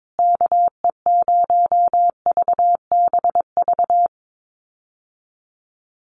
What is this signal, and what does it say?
Morse code, heard as a keyed tone, about 22 wpm, 700 Hz. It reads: KE0VBV
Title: USA Call Signs Repeated 3x 22wpm